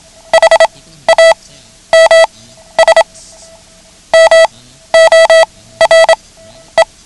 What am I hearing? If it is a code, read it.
Morse code, heard as a keyed tone, about 27 wpm, spaced as for 13 wpm, 650 Hz. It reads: HAMS MORE
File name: ham radio / CodeQuick / Code Quick 09